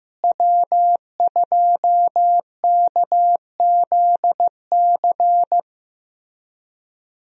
Morse code, heard as a keyed tone, about 14 wpm, 700 Hz. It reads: W2KZC
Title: USA Call Signs - 5 Characters - Repeated 3x - 15wpm